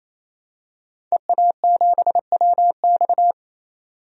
Morse code, heard as a keyed tone, about 28 wpm, 700 Hz. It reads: EA7WX
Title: DXCC-100 Call Signs - 5 Characters - 28wpm